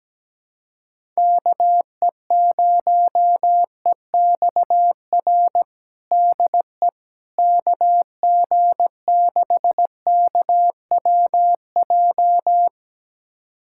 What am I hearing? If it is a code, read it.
Morse code, heard as a keyed tone, about 17 wpm, 700 Hz. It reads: KE0EXR DE KG6KWJ
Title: USA Callsign DE Callsign - 6 Characters - 17wpm